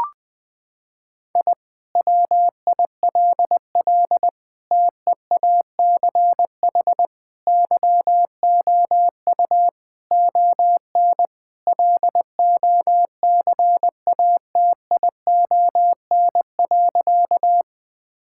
Morse code, heard as a keyed tone, about 19 wpm, 700 Hz. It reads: I WILL TEACH YOU ON LOCATION.
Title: Sentences from Top 1000 Words - Review Part 6 - 20wpm